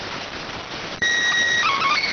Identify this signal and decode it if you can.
SSTV (slow-scan television), PD 290.